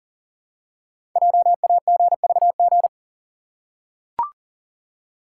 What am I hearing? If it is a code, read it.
Morse code, heard as a keyed tone, about 40 wpm, 700 Hz. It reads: JAGVG
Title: Code Groups - 5 Letters - 40wpm